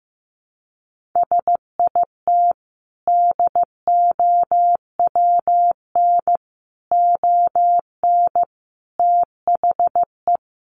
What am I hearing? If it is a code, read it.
Morse code, heard as a keyed tone, about 14 wpm, 700 Hz. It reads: SIT DOWN ON THE
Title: Sentences from Top 1400 Words 15wpm